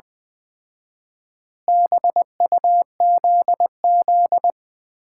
Morse code, heard as a keyed tone, about 21 wpm, 700 Hz. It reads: BUZZ
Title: Common QSO Phrases - 4x Word-Spacing - 20wpm